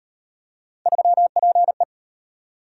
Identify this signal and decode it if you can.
Morse code, heard as a keyed tone, about 36 wpm, 700 Hz. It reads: 3PE